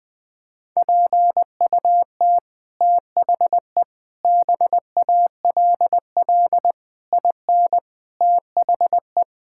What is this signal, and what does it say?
Morse code, heard as a keyed tone, about 20 wpm, 700 Hz. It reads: PUT THE BALL IN THE
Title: Sets of 9 Words - 20wpm